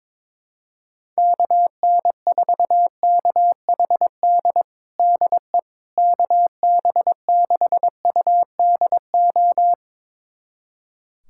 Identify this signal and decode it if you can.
Morse code, heard as a keyed tone, about 22 wpm, 700 Hz. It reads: KN4KHD DE KB6UDO
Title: USA Callsign DE Callsign 22wpm